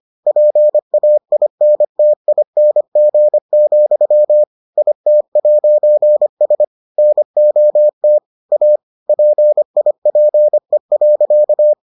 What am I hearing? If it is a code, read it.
Morse code, heard as a keyed tone, about 25 wpm, 600 Hz. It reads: PAINTING, IT'S NOT A PIPE.